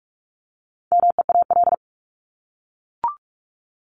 Morse code, heard as a keyed tone, about 44 wpm, 700 Hz. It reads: MEAL
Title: Top 1751-2000 Words 45wpm